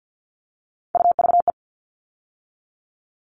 Morse code, heard as a keyed tone, about 50 wpm, 700 Hz. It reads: UVE
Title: All Cut Numbers - 3-Digit 50wpm